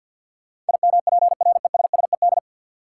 Morse code, heard as a keyed tone, about 51 wpm, 700 Hz. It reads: IMPRESSED
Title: Sentences from Top 1600 Words - New Word Drill - 50wpm